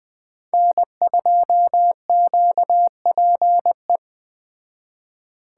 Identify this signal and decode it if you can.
Morse code, heard as a keyed tone, about 20 wpm, 700 Hz. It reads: N2QPE